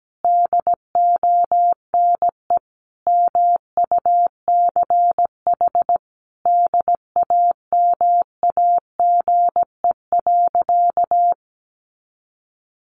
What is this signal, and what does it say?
Morse code, heard as a keyed tone, about 17 wpm, 700 Hz. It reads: DONE MUCH DAMAGE.
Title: Sentences from Top 1100 Words 17wpm